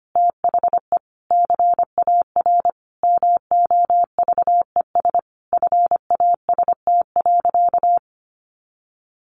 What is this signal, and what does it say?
Morse code, heard as a keyed tone, about 25 wpm, 700 Hz. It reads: THE CAR MOVES FAST.
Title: Easy Phrases 25wpm